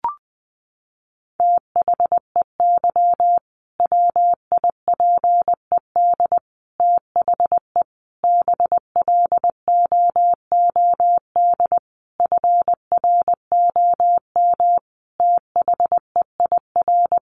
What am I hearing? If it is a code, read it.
Morse code, heard as a keyed tone, about 19 wpm, 700 Hz. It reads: THEY WIPED THE BLOOD FROM THEIR
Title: A Sound Of Thunder 20wpm